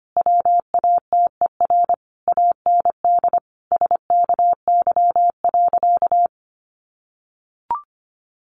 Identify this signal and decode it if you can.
Morse code, heard as a keyed tone, about 25 wpm, 700 Hz. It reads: WATER AND SKY.
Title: Sentences from Top 1200 Words 25wpm